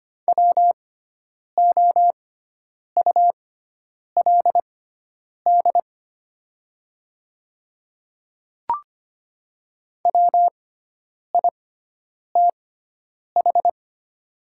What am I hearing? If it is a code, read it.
Morse code, heard as a keyed tone, about 25 wpm, spaced as for 9 wpm, 700 Hz. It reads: WOULD WITH
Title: 6x Character Spacing - Top 100 Words - 25wpm